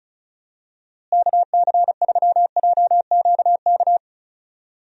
Morse code, heard as a keyed tone, about 34 wpm, 700 Hz. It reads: KC3JQK
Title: USA Call Signs - 6 Characters - Repeated 3x - 35wpm